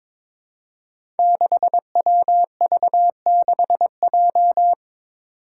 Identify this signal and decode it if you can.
Morse code, heard as a keyed tone, about 23 wpm, 700 Hz. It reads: 6WV6J
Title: Five Challenging Letter-Number Combinations 22wpm